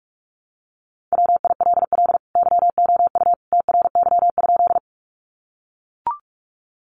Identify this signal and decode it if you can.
Morse code, heard as a keyed tone, about 44 wpm, 700 Hz. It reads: WILL YOU TRY?
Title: Sentences from Top 200 Words 45wpm